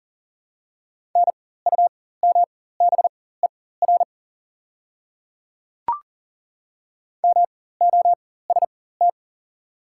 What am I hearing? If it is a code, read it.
Morse code, heard as a keyed tone, about 40 wpm, spaced as for 19 wpm, 700 Hz. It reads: NUMBER MOST
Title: 4x Character Spacing - Top 200 Words - 40wpm